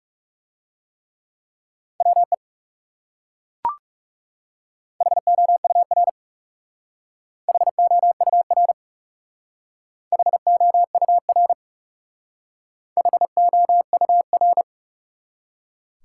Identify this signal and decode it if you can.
Morse code, heard as a keyed tone, about 39 wpm, 700 Hz. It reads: WE HOUR HOUR HOUR HOUR